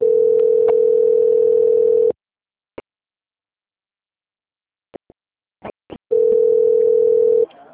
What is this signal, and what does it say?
Ringback tone (North America), 440 + 480 Hz.